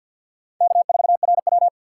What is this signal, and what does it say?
Morse code, heard as a keyed tone, about 53 wpm, 700 Hz. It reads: K4RW
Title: USA Call Signs - 2x Character Spacing - 50wpm